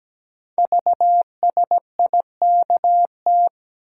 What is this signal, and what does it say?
Morse code, heard as a keyed tone, about 17 wpm, 700 Hz. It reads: VSIKT
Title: Code Groups - 5 Letters - 17wpm